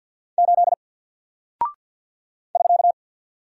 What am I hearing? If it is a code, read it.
Morse code, heard as a keyed tone, about 50 wpm, 700 Hz. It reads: 8 <SK>
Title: Single Character (Letter:Number:Punctuation:Prosign) 50wpm